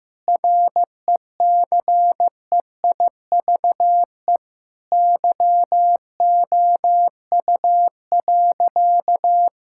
Morse code, heard as a keyed tone, about 15 wpm, 700 Hz. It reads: RECEIVE YOU.
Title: Sentences from Top 1600 Words 15wpm